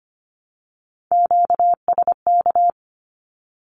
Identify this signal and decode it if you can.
Morse code, heard as a keyed tone, about 25 wpm, 700 Hz. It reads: QSK
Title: Q-Codes Repeated 3x 25wpm